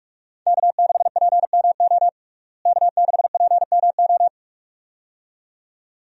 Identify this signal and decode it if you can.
Morse code, heard as a keyed tone, about 46 wpm, 700 Hz. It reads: K6PMO K6PMO
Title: USA Call Signs - 5 Characters - Repeated 2x - 45wpm